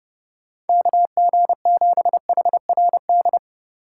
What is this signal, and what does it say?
Morse code, heard as a keyed tone, about 29 wpm, 700 Hz. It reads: KG7HRD